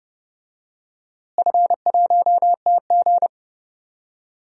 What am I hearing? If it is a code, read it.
Morse code, heard as a keyed tone, about 32 wpm, 700 Hz. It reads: F1TG
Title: DXCC-100 Call Signs - 4 Characters - Repeated 2x - 30wpm